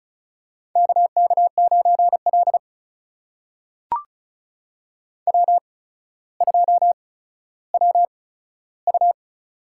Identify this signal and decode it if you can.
Morse code, heard as a keyed tone, about 36 wpm, spaced as for 15 wpm, 700 Hz. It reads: KK9L W2WU
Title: USA Call Signs - 8x Character Spacing - 35wpm